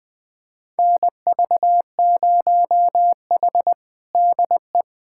Morse code, heard as a keyed tone, about 20 wpm, 700 Hz. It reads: NV0H DE
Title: USA Callsign DE Callsign - 4 Characters - 20wpm